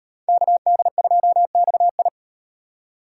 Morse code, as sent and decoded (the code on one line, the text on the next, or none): -.- -.. ..--- -..- ..
KD2XI